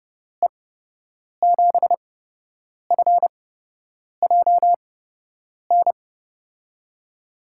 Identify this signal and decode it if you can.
Morse code, heard as a keyed tone, about 30 wpm, spaced as for 8 wpm, 700 Hz. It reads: E7FJN
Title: USA Call Signs - 6 Characters - 8x Character Spacing - 30wpm